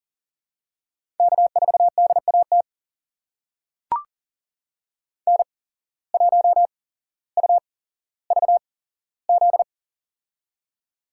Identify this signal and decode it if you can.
Morse code, heard as a keyed tone, about 40 wpm, spaced as for 17 wpm, 700 Hz. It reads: K4DAT N1UVZ
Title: USA Call Signs - 5 Characters - 8x Character Spacing - 40wpm